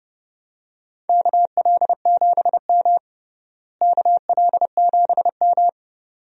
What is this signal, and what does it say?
Morse code, heard as a keyed tone, about 30 wpm, 700 Hz. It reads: KL7M KL7M
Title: DXCC-100 Call Signs - 4 Characters - Repeated 2x - 30wpm